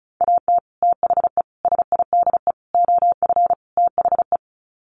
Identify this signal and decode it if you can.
Morse code, heard as a keyed tone, about 34 wpm, 700 Hz. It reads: AT THE SIDE OF THE
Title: Sets of 5 Words Encore2 35wpm